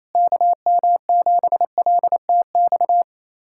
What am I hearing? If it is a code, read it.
Morse code, heard as a keyed tone, about 28 wpm, 700 Hz. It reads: KM7LTX
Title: USA Callsign DE Callsign 28wpm